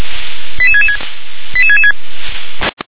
Five-tone selective call (ZVEI), heard as quick runs of five tones, tones of about 80 ms. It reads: 88666 88666